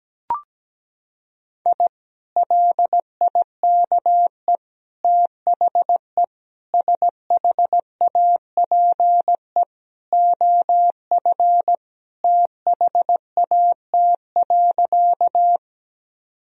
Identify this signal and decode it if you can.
Morse code, heard as a keyed tone, about 17 wpm, 700 Hz. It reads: I LIKE THE SHAPE OF THAT.
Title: Sentences from Top 1000 Words - Review Part 3 - 17wpm